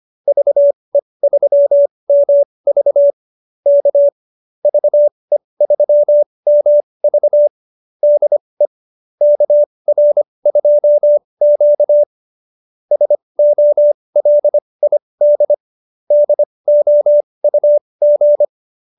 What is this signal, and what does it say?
Morse code, heard as a keyed tone, about 25 wpm, spaced as for 20 wpm, 570 Hz. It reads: VE3MV K VE3MV DE KR2Q SOLID DOUG